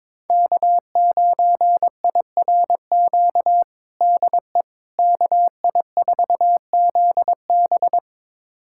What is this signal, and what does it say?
Morse code, heard as a keyed tone, about 22 wpm, 700 Hz. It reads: K9IRQ DE KI4ZB
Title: USA Callsign DE Callsign - 5 Characters - 22wpm